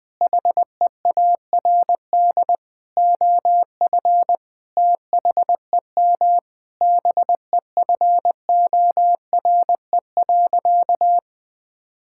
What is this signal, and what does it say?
Morse code, heard as a keyed tone, about 20 wpm, 700 Hz. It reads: HEARD OF THEM BEFORE.
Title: Sentences with 9-Word Phrases - Spoken Beginning - 20wpm